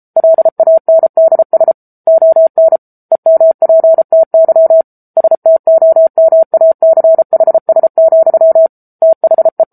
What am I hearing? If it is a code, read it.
Morse code, heard as a keyed tone, about 34 wpm, 650 Hz. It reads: LANDS ON EMPTY STOMACHS, THE
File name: Book Two / 07-Chapter 7 - 33 WPM